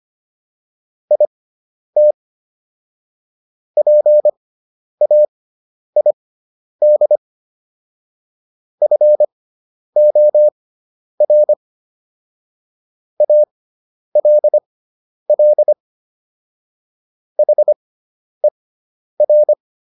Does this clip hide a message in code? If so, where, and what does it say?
Yes: Morse code from 1.1 to 19.5 s, heard as a keyed tone, about 25 wpm, spaced as for 10 wpm, 600 Hz. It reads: IT PAID FOR ALL HER